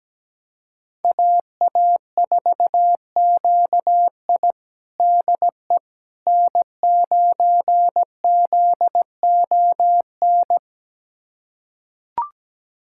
Morse code, heard as a keyed tone, about 17 wpm, 700 Hz. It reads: AA4QI DE N9ZON
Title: USA Callsign DE Callsign - 5 Characters - 17wpm